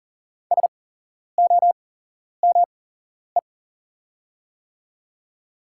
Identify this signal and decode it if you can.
Morse code, heard as a keyed tone, about 40 wpm, spaced as for 11 wpm, 700 Hz. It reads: SOME
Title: 8x Character Spacing - Top 100 Words - 40wpm